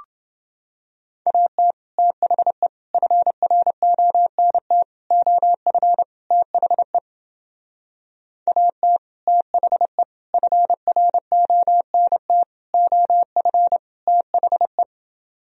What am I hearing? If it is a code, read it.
Morse code, heard as a keyed tone, about 30 wpm, 700 Hz. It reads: AT THE FRONT OF THE AT THE FRONT OF THE